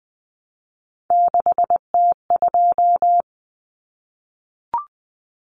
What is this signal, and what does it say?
Morse code, heard as a keyed tone, about 20 wpm, 700 Hz. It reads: 6T2
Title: Common Cut Numbers - 3-Digit 20wpm